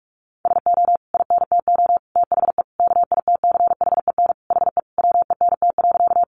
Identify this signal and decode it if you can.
Morse code, heard as a keyed tone, about 44 wpm, 700 Hz. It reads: SO INTO THE KITCHEN HE WENT.
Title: Sentences from Top 1400 Words 45wpm